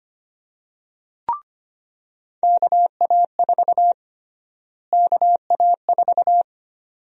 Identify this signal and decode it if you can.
Morse code, heard as a keyed tone, about 25 wpm, 700 Hz. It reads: KA4 KA4
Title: USA Call Sign Prefixes - Repeated 2x - 25wpm